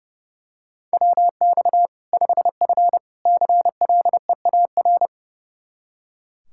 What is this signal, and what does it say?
Morse code, heard as a keyed tone, about 32 wpm, 700 Hz. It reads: WX 5F CLEAR
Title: Practice QSO - Exchange 2 - Info - 30wpm